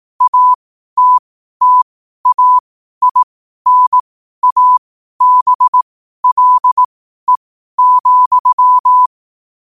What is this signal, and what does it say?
Morse code, heard as a keyed tone, about 18 wpm, spaced as for 12 wpm, 1000 Hz. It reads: ATTAINABLE,
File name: qotdfiles / 12wpm